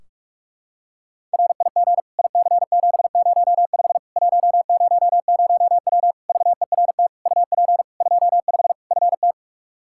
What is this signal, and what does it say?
Morse code, heard as a keyed tone, about 41 wpm, 700 Hz. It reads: RIG IC705 100W VERT UP 25 FT
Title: K1USN SST QSO Practice 45wpm